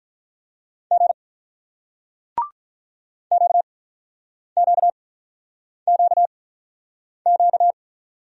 Morse code, heard as a keyed tone, about 48 wpm, spaced as for 9 wpm, 700 Hz. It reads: G QQQQ